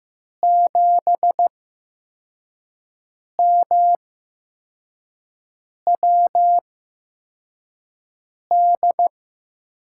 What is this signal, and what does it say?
Morse code, heard as a keyed tone, about 15 wpm, spaced as for 4 wpm, 700 Hz. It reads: 7MWD
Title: USA Call Signs - 5 Characters - 8x Character Spacing - 15wpm